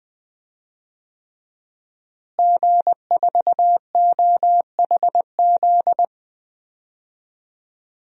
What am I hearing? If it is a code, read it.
Morse code, heard as a keyed tone, about 21 wpm, 700 Hz. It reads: G4OHZ
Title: DXCC-100 Call Signs - 5 Characters - Repeated 3x - 20wpm